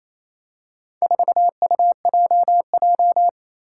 Morse code, heard as a keyed tone, about 27 wpm, 700 Hz. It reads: 4UJJ
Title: Four Challenging Letter-Number Combinations 28wpm